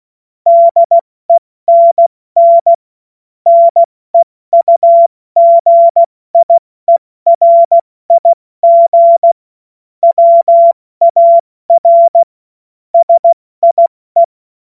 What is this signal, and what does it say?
Morse code, heard as a keyed tone, about 16 wpm, 680 Hz. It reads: DENN NEUGIERIG WAR SIE
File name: Morsekurs / Witze / Witz 09. Dezember 2025 70 F80